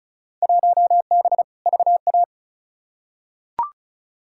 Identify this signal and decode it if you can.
Morse code, heard as a keyed tone, about 36 wpm, 690 Hz. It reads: 1B VA
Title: Field Day Exchanges Encore - 35wpm